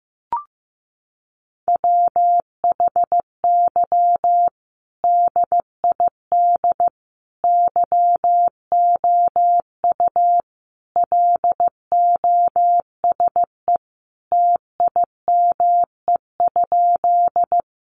Morse code, heard as a keyed tone, about 15 wpm, 700 Hz. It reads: WHY DID YOU LOSE TIME?
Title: Sentences from Top 1100 Words 15wpm